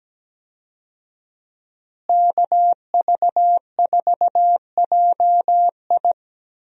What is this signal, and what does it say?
Morse code, heard as a keyed tone, about 17 wpm, 700 Hz. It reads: KV4JI
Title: USA Call Signs - 5 Characters - Repeated 3x - 17wpm